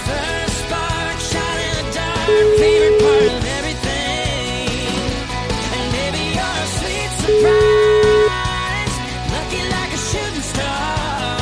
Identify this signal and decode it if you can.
Ringback tone (Europe, 425 Hz), with noise.